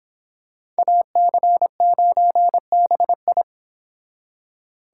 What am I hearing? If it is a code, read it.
Morse code, heard as a keyed tone, about 26 wpm, 700 Hz. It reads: AC9BI